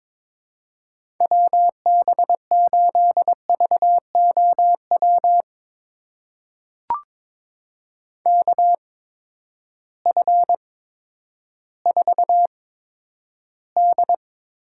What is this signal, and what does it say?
Morse code, heard as a keyed tone, about 22 wpm, spaced as for 11 wpm, 700 Hz. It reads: WB8VOW KF4D
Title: USA Call Signs - 6 Characters - 8x Character Spacing - 22wpm